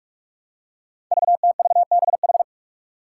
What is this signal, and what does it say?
Morse code, heard as a keyed tone, about 44 wpm, 700 Hz. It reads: VTVBH
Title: Five Challenging Letter-Number Combinations 45wpm